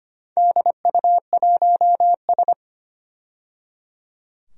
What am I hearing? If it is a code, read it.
Morse code, heard as a keyed tone, about 25 wpm, 700 Hz. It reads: DU1S
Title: DXCC-100 Call Signs - 4 Characters - 25wpm